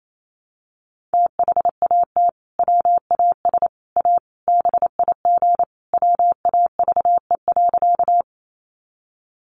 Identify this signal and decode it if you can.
Morse code, heard as a keyed tone, about 28 wpm, 700 Hz. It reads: THAT WAS A BIG WAVE.
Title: Sentences from Top 1400 Words 28wpm